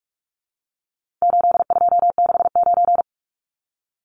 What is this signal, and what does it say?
Morse code, heard as a keyed tone, about 44 wpm, 700 Hz. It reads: 8269
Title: Numbers - 4-Digit 45wpm